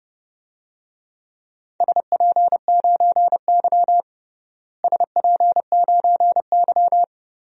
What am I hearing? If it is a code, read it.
Morse code, heard as a keyed tone, about 30 wpm, 700 Hz. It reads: SP9Y SP9Y
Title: DXCC-100 Call Signs - 4 Characters - Repeated 2x - 30wpm